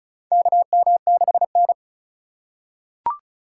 Morse code, heard as a keyed tone, about 34 wpm, 700 Hz. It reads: KM6N
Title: USA Call Signs - 4 Characters - 2x Character Spacing - 35wpm